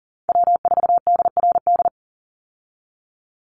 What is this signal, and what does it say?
Morse code, heard as a keyed tone, about 40 wpm, 700 Hz. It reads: W4DRD
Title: USA Call Signs 40wpm